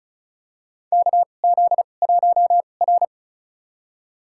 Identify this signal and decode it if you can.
Morse code, heard as a keyed tone, about 36 wpm, spaced as for 25 wpm, 700 Hz. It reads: KZ1R
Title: USA Call Signs - 4 Characters - 2x Character Spacing - 35wpm